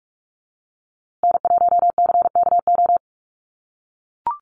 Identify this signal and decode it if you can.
Morse code, heard as a keyed tone, about 44 wpm, 700 Hz. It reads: N1CKO
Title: USA Call Signs Repeated 3x 45wpm